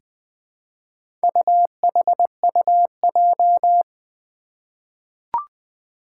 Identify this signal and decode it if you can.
Morse code, heard as a keyed tone, about 20 wpm, 700 Hz. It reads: UHUJ
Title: Four Challenging Letter-Number Combinations 20wpm